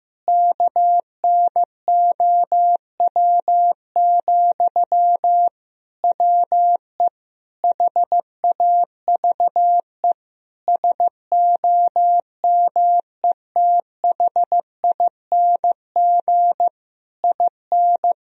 Morse code, heard as a keyed tone, about 15 wpm, 700 Hz. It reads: KNOW, WE HAVE SOMETHING IN
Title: Sentences from Top 1000 Words - Review Part 3 - 15wpm